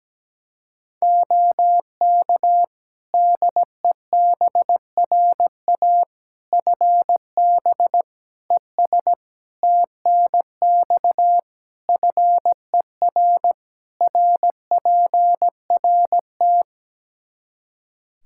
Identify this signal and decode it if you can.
Morse code, heard as a keyed tone, about 17 wpm, 700 Hz. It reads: OK DEBRA FB ES TNX FER RPRT
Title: Practice QSO - Exchange 2 - Info - 17wpm